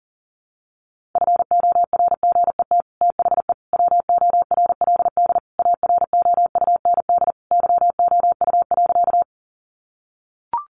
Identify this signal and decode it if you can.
Morse code, heard as a keyed tone, about 37 wpm, 700 Hz. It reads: FORGET THE WORLD AROUND YOU.
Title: Sentences from Top 1200 Words 40wpm